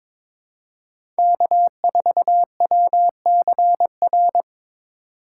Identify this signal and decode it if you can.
Morse code, heard as a keyed tone, about 22 wpm, 700 Hz. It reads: K4WCR
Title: USA Call Signs - 5 Characters - 2x Character Spacing - 22wpm